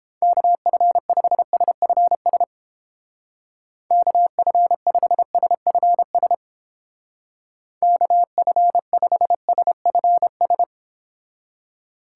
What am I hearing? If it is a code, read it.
Morse code, heard as a keyed tone, about 29 wpm, 700 Hz. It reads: KF5SFS KF5SFS KF5SFS